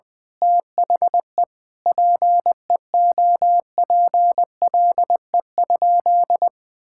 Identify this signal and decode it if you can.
Morse code, heard as a keyed tone, about 20 wpm, 700 Hz. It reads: THE PEOPLE?
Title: Sentences from Top 1000 Words - Review Part 5 - 20wpm